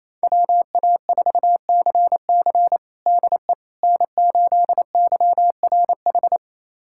Morse code, heard as a keyed tone, about 28 wpm, 700 Hz. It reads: WA4CC DE N8YRH
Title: USA Callsign DE Callsign - 5 Characters - 28wpm